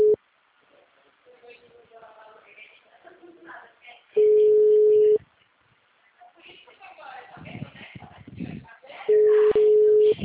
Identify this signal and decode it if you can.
Ringback tone (Europe, 425 Hz), with speech.
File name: out- -2028- - - .81060